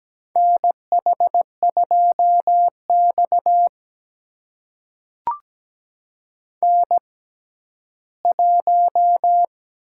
Morse code, heard as a keyed tone, about 17 wpm, spaced as for 12 wpm, 700 Hz. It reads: NH2X N1
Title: USA Call Signs - 4 Characters - 6x Character Spacing - 17wpm